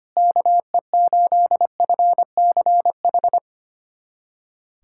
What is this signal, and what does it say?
Morse code, heard as a keyed tone, about 25 wpm, 700 Hz. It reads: KE8FCH
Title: USA Callsign DE Callsign - 6 Characters - 25wpm